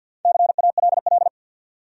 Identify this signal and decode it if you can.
Morse code, heard as a keyed tone, about 53 wpm, 700 Hz. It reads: CALL
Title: Top 100 Words 50wpm